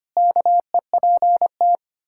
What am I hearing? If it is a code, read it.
Morse code, heard as a keyed tone, about 25 wpm, 700 Hz. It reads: KEPT
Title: Pangram Challenge v2 - 25wpm